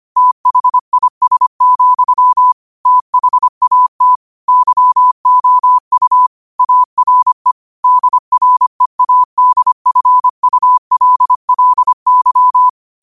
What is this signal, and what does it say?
Morse code, heard as a keyed tone, about 25 wpm, 1000 Hz. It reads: THIS, THAT YOU ARE DREADFULLY
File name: qotdfiles / 25wpm